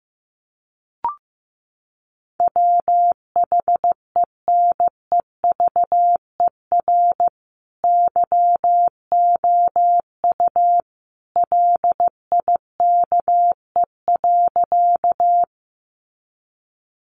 Morse code, heard as a keyed tone, about 15 wpm, 700 Hz. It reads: WHENEVER YOU LIKE.
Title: Sentences from Top 1200 Words 15wpm